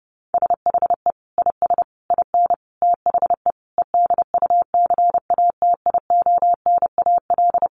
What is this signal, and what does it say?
Morse code, heard as a keyed tone, about 30 wpm, 700 Hz. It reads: SHE IS IN THE EDUCATIONAL
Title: Sentences from Top 1200 Words 30wpm